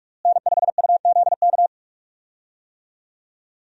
Morse code, heard as a keyed tone, about 46 wpm, 700 Hz. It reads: N5UZK
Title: USA Call Signs - 5 Characters - Repeated 2x - 45wpm